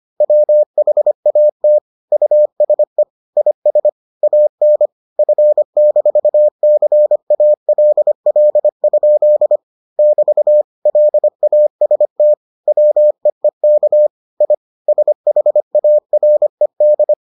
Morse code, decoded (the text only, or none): WHAT USE IS AN F-CALL? <BT> LAST WEEK I SHARED